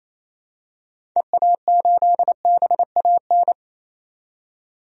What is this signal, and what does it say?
Morse code, heard as a keyed tone, about 27 wpm, 700 Hz. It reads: EA8BAN
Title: DXCC-100 Call Signs - 6 Characters - 28wpm